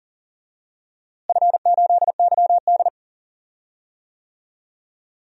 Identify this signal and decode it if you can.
Morse code, heard as a keyed tone, about 40 wpm, 700 Hz. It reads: F8YD
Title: DXCC-100 Call Signs - 4 Characters - Repeated 2x - 40wpm